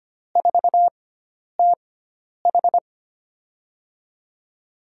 Morse code, heard as a keyed tone, about 25 wpm, 700 Hz. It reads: 4 T H
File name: Lesson 16g - Letter c - Call Signs 25wpm